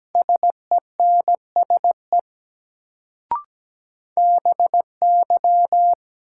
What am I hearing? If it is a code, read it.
Morse code, heard as a keyed tone, about 17 wpm, 700 Hz. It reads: SENSE BY
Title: Sets of 2 Words - New Word Drill - 17wpm